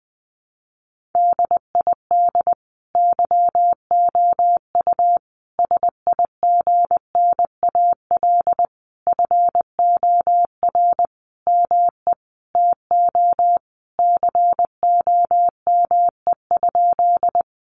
Morse code, heard as a keyed tone, about 20 wpm, 700 Hz. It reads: DID YOU SIGNAL FOR ME TO COME?
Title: Sentences from Top 1400 Words 20wpm